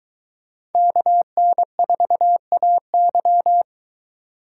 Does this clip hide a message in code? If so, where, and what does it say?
Yes: Morse code from 0.8 to 3.6 s, heard as a keyed tone, about 21 wpm, 700 Hz. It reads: KN4AY